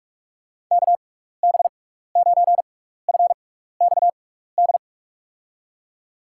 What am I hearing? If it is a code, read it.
Morse code, heard as a keyed tone, about 43 wpm, spaced as for 15 wpm, 700 Hz. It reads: KB9FXD
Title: USA Call Signs - 6 Characters - 6x Character Spacing - 45wpm